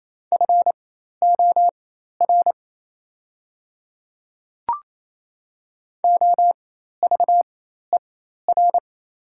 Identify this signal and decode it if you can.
Morse code, heard as a keyed tone, about 28 wpm, spaced as for 13 wpm, 690 Hz. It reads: FOR OVER
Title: 4x Character Spacing - Top 100 Words - 28wpm